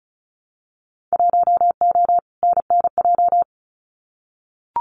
Morse code, heard as a keyed tone, about 36 wpm, 700 Hz. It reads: 1O NNJ
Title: Winter Field Day Exchanges 35wpm